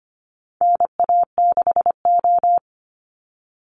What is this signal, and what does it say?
Morse code, heard as a keyed tone, about 25 wpm, 700 Hz. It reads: NA6O
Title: Call Signs - 25wpm